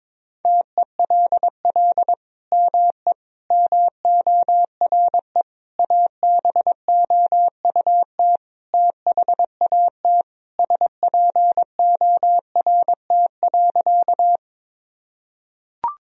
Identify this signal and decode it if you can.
Morse code, heard as a keyed tone, about 22 wpm, 700 Hz. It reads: TELL ME MORE ABOUT THAT SPORT.
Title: Sentences from Top 1000 Words - Review Part 4 - 22wpm